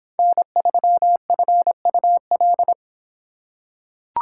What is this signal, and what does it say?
Morse code, heard as a keyed tone, about 24 wpm, 700 Hz. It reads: N3FUL